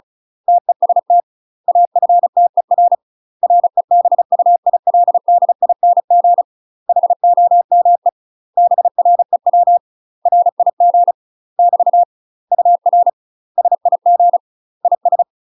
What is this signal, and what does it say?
Morse code, heard as a keyed tone, about 36 wpm, 700 Hz. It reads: TEST AFTER REBUILDING HOME BREW RIG <BT> UR SIG IS
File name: UK MASTER QSOs TRAINING / 35WPM / UK MASTER QSOs 0014